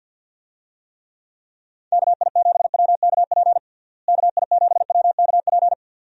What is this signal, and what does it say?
Morse code, heard as a keyed tone, about 51 wpm, 700 Hz. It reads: KI7WKP KI7WKP
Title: USA Call Signs - 6 Characters - Repeated 2x - 50wpm